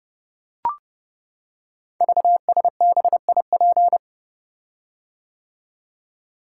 Morse code, heard as a keyed tone, about 30 wpm, 700 Hz. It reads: VSBIP
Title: Code Groups - 5 Letter:Numbers - 30wpm